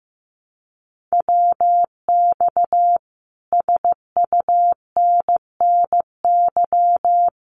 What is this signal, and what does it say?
Morse code, heard as a keyed tone, about 15 wpm, 700 Hz. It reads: WX SUNNY
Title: Common QSO Phrases Encore 15wpm